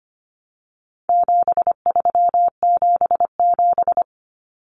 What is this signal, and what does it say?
Morse code, heard as a keyed tone, about 25 wpm, 700 Hz. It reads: 7377
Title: Common Cut Numbers - 4-Digit 25wpm